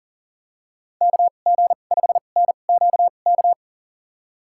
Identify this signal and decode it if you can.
Morse code, heard as a keyed tone, about 39 wpm, spaced as for 29 wpm, 700 Hz. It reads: KG5NQK
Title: USA Call Signs - 6 Characters - 2x Character Spacing - 40wpm